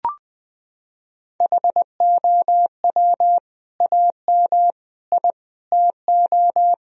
Morse code, heard as a keyed tone, about 21 wpm, 700 Hz. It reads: HOW AM I TO
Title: Sentences from Top 1000 Words - Review Part 4 - 20wpm